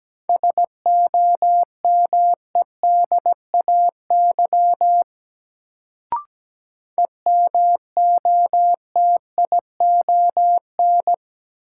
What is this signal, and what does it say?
Morse code, heard as a keyed tone, about 17 wpm, 700 Hz. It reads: SOMEDAY EMOTION
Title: Sentences from Top 1600 Words - New Word Drill Alt - 17wpm